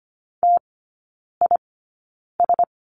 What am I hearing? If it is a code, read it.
Morse code, heard as a keyed tone, about 25 wpm, spaced as for 9 wpm, 700 Hz. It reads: TIS
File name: Lesson 17e - Letter u - Three Characters 25wpm